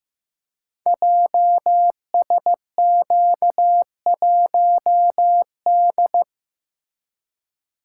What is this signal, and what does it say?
Morse code, heard as a keyed tone, about 15 wpm, 700 Hz. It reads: JSQ1D
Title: Code Groups - 5 Letter:Numbers - 15wpm